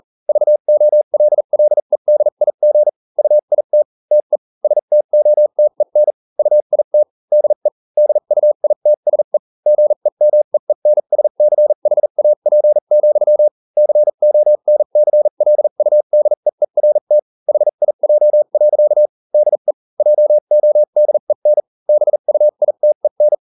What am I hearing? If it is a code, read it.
Morse code, heard as a keyed tone, about 40 wpm, spaced as for 35 wpm, 600 Hz. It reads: VOLLEDIG UIT TE STOTEN UIT DE DUITSE GEMEENSCHAP, CONCLUDEERT HIJ. DE JODEN BUITEN